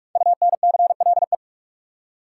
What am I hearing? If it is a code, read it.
Morse code, heard as a keyed tone, about 44 wpm, 700 Hz. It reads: UNCLE
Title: Five Letter Words - 45wpm